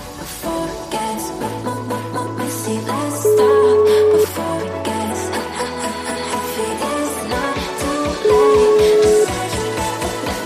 Ringback tone (Europe, 425 Hz), with noise.